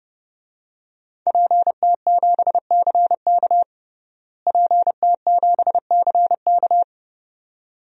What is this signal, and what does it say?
Morse code, heard as a keyed tone, about 30 wpm, 700 Hz. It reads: PT7CK PT7CK
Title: DXCC-100 Call Signs - 5 Characters - Repeated 2x - 30wpm